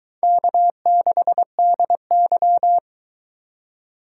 Morse code, heard as a keyed tone, about 22 wpm, 700 Hz. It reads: K6DY